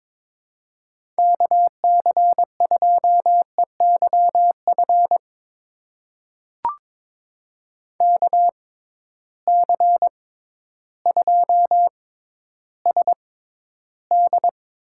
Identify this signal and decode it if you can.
Morse code, heard as a keyed tone, about 22 wpm, spaced as for 12 wpm, 700 Hz. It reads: KC2EYF KC2SD
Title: USA Call Signs - 6 Characters - 6x Character Spacing - 22wpm